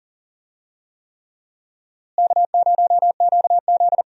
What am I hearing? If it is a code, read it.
Morse code, heard as a keyed tone, about 40 wpm, 700 Hz. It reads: K0QZ
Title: USA Call Signs - 4 Characters - Repeated 3x - 40wpm